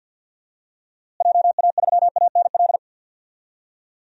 Morse code, heard as a keyed tone, about 53 wpm, 700 Hz. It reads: JA3ANL
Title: DXCC-100 Call Signs - 6 Characters - Repeated 2x - 50wpm